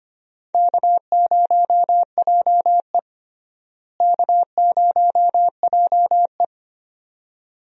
Morse code, heard as a keyed tone, about 25 wpm, 700 Hz. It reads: K0JE K0JE
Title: USA Call Signs - 4 Characters - Repeated 2x - 25wpm